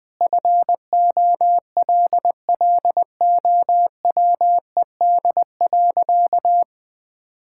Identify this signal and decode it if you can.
Morse code, heard as a keyed tone, about 20 wpm, 700 Hz. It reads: FOLLOWED.
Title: Sentences with 5-Word Phrases - Spoken Beginning - 20wpm